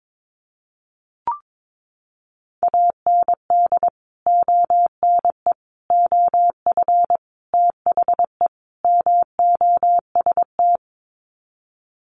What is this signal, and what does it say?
Morse code, heard as a keyed tone, about 22 wpm, 700 Hz. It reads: AND ONE OF THE MOST
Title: Sets of 5 Words Encore 22wpm